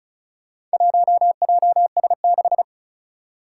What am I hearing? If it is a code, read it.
Morse code, heard as a keyed tone, about 36 wpm, 700 Hz. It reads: 1JS6
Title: Four Challenging Letter-Number Combinations 35wpm